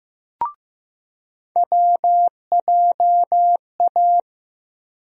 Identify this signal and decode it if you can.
Morse code, heard as a keyed tone, about 15 wpm, 700 Hz. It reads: WJA
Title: USA Call Sign Postfixes - 3 Letters 15wpm